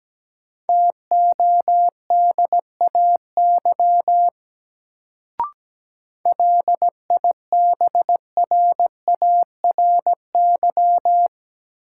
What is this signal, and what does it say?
Morse code, heard as a keyed tone, about 17 wpm, 700 Hz. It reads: TODAY LIBRARY
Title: Sentences from Top 600 Words - New Word Drill - 17wpm